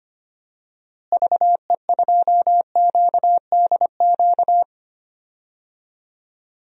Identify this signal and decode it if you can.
Morse code, heard as a keyed tone, about 25 wpm, 700 Hz. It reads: VE2QDQ
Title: DXCC-100 Call Signs - 6 Characters - Repeated 2x - 25wpm